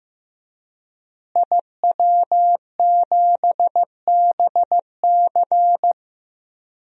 Morse code, heard as a keyed tone, about 15 wpm, 700 Hz. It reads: IW7BC
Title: DXCC-100 Call Signs - 5 Characters - Repeated 3x - 15wpm